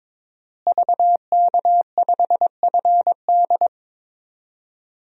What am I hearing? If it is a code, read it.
Morse code, heard as a keyed tone, about 22 wpm, 700 Hz. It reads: VK5FD
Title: DXCC-100 Call Signs - 5 Characters - Repeated 3x - 22wpm